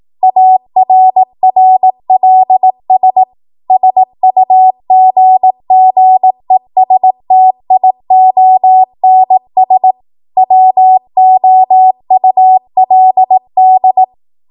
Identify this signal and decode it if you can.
Morse code, heard as a keyed tone, about 18 wpm, 750 Hz. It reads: ARRLS SUGGESTIONS WOULD
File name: Morse / 18WPM